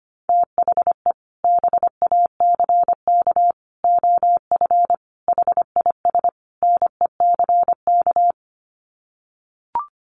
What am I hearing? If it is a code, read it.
Morse code, heard as a keyed tone, about 25 wpm, 700 Hz. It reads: THE BACK OF HIS NECK
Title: Sets of 5 Words Encore 25wpm